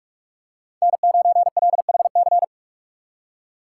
Morse code, heard as a keyed tone, about 47 wpm, 700 Hz. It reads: N9LHC
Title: USA Call Signs - 5 Characters - 4x Character Spacing - 45wpm